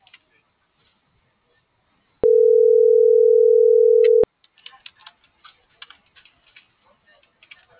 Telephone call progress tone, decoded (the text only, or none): ringback tone